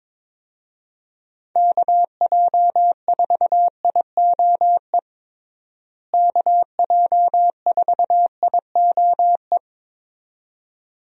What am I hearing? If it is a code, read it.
Morse code, heard as a keyed tone, about 21 wpm, 700 Hz. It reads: KJ4IOE KJ4IOE
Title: USA Call Signs - 6 Characters - Repeated 2x - 22wpm